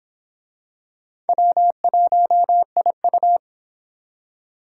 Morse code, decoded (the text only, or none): W1IU